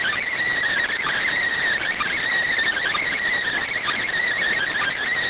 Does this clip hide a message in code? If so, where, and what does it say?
Yes: Emergency Alert System (SAME) tones from 0.9 to 5.0 s, heard as harsh buzzing bursts.